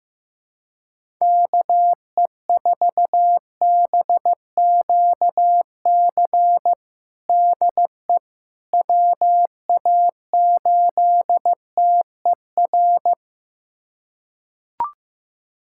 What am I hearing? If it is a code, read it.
Morse code, heard as a keyed tone, about 15 wpm, 700 Hz. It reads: KE4BQC DE WA8TER
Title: USA Callsign DE Callsign 15wpm